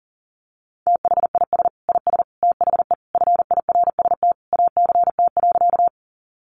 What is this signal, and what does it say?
Morse code, heard as a keyed tone, about 40 wpm, 700 Hz. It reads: THIS IS THE FIRST ACT.
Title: Sentences from Top 200 Words 40wpm